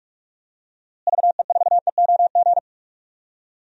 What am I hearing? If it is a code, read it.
Morse code, heard as a keyed tone, about 46 wpm, 700 Hz. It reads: VE4EOG
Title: DXCC-100 Call Signs - 6 Characters - Repeated 3x - 45wpm